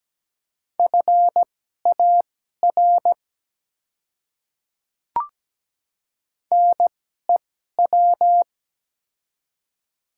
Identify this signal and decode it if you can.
Morse code, heard as a keyed tone, about 17 wpm, spaced as for 12 wpm, 700 Hz. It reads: FAR NEW
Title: 2x Character Spacing - Top 300 Words - 17wpm